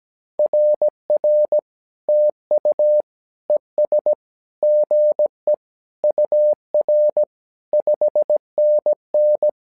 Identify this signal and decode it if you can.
Morse code, heard as a keyed tone, about 17 wpm, 600 Hz. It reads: RR TU ES GE UR 5NN
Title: SOTA QSO Practice 17wpm